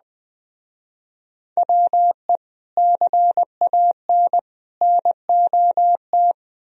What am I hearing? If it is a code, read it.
Morse code, heard as a keyed tone, about 20 wpm, 700 Hz. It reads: WE CAN NOT
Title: Speed-Racing - Sets of 3 Words 17wpm